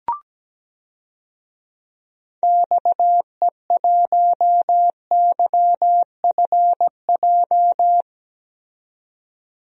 Morse code, heard as a keyed tone, about 17 wpm, 700 Hz. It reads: XE1YFJ
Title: DXCC-100 Call Signs - 6 Characters - 17wpm